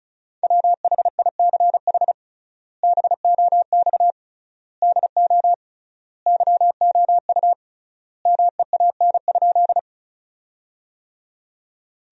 Morse code, heard as a keyed tone, about 36 wpm, 700 Hz. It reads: WHICH BOX DO YOU MEAN?
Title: 3x Word-Spacing - Sentences from Top 500 Words - 35wpm